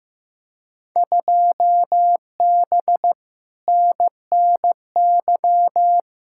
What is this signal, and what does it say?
Morse code, heard as a keyed tone, about 15 wpm, 700 Hz. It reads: 2B NNY
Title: Field Day Exchanges Encore - 15wpm